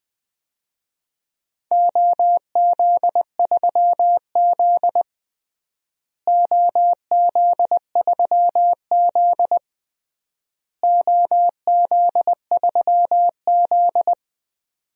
Morse code, heard as a keyed tone, about 20 wpm, 700 Hz. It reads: OZ3Z OZ3Z OZ3Z
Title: DXCC-100 Call Signs - 4 Characters - Repeated 3x - 20wpm